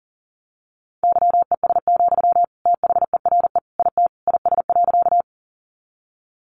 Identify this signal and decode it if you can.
Morse code, heard as a keyed tone, about 40 wpm, 700 Hz. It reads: YES, THERE IT IS.
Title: Sentences from Top 500 Words 40wpm